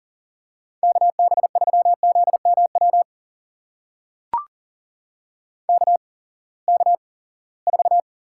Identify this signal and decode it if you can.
Morse code, heard as a keyed tone, about 39 wpm, spaced as for 22 wpm, 700 Hz. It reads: KB3ZMW KK4
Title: USA Call Signs - 6 Characters - 8x Character Spacing - 40wpm